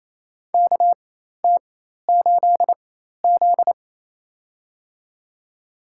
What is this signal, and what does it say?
Morse code, heard as a keyed tone, about 28 wpm, spaced as for 13 wpm, 700 Hz. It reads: KT8Z
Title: USA Call Signs - 4 Characters - 4x Character Spacing - 28wpm